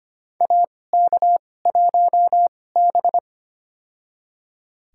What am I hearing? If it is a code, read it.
Morse code, heard as a keyed tone, about 25 wpm, spaced as for 18 wpm, 700 Hz. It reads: AK1B
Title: USA Call Signs - 2x Character Spacing - 25wpm